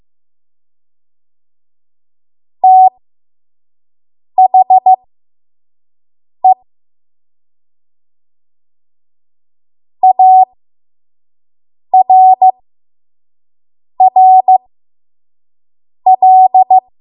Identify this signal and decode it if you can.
Morse code, heard as a keyed tone, about 15 wpm, spaced as for 5 wpm, 750 Hz. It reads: THE ARRL
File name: Morse / 05WPM